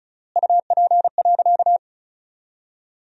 Morse code, heard as a keyed tone, about 34 wpm, 700 Hz. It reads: UP.
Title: 3x Word-Spacing - Sentences from Top 500 Words - 35wpm